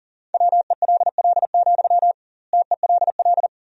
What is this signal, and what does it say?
Morse code, heard as a keyed tone, about 40 wpm, 700 Hz. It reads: WELL, TELL
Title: 2x Word-Spacing - Sentences from Top 200 Words - 40wpm